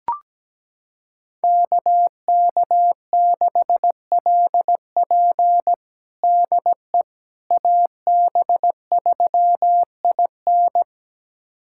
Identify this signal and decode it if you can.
Morse code, heard as a keyed tone, about 17 wpm, 700 Hz. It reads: KK6LP DE AB3IN
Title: USA Callsign DE Callsign - 5 Characters - 17wpm